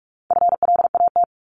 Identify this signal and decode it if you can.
Morse code, heard as a keyed tone, about 44 wpm, 700 Hz. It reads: FLAT
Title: Top 751-1000 Words 45wpm